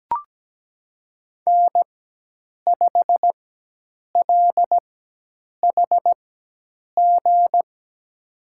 Morse code, heard as a keyed tone, about 17 wpm, spaced as for 8 wpm, 700 Hz. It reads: N5LHG
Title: USA Call Signs - 4x Character Spacing - 17wpm